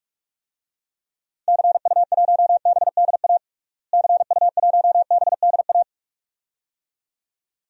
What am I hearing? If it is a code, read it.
Morse code, heard as a keyed tone, about 46 wpm, 700 Hz. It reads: CU1BDA CU1BDA
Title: DXCC-100 Call Signs - 6 Characters - Repeated 2x - 45wpm